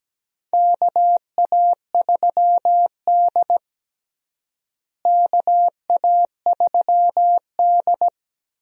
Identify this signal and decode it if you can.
Morse code, heard as a keyed tone, about 17 wpm, 700 Hz. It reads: KA3D KA3D
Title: USA Call Signs - 4 Characters - Repeated 2x - 17wpm